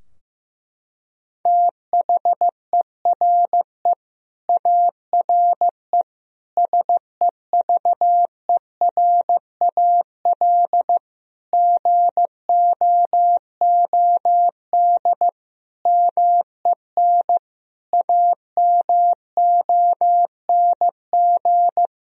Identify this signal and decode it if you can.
Morse code, heard as a keyed tone, about 16 wpm, 700 Hz. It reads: THERE ARE SEVERAL GOOD MEN AMONG
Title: Sentences from Top 1000 Words - Review Part 2 - 15wpm